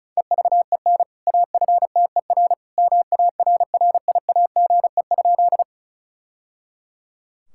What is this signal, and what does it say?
Morse code, heard as a keyed tone, about 36 wpm, 700 Hz. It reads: EVEN AFTER MARRIAGE?
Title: Sentences from Top 1000 Words - Review Part 5 - 35wpm